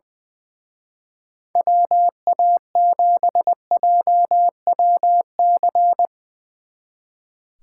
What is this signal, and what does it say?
Morse code, heard as a keyed tone, about 20 wpm, 700 Hz. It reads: WA7JWC
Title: Speed Racing - USA 2x3 Call Signs 20wpm